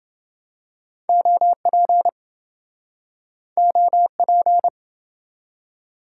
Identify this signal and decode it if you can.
Morse code, heard as a keyed tone, about 28 wpm, 700 Hz. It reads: OP OP